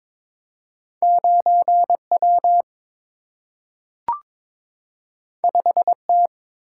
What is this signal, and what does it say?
Morse code, heard as a keyed tone, about 22 wpm, 700 Hz. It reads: 9 W 5 T
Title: Single-Letter Single-Number 22wpm